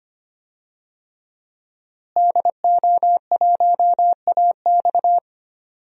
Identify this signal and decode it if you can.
Morse code, heard as a keyed tone, about 25 wpm, 700 Hz. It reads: DO1AX
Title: DXCC-100 Call Signs - 5 Characters - Repeated 3x - 25wpm